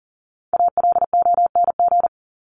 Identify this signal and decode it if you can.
Morse code, heard as a keyed tone, about 40 wpm, 700 Hz. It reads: ALONG
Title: Top 501-750 Words 40wpm